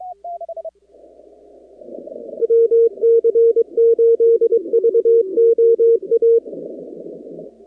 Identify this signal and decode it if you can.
Morse code, heard as a keyed tone, about 23 wpm, 460 Hz. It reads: WC8VOA
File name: contests / 2019-cw-wpx-kt5j-k5pi- -2330 0-45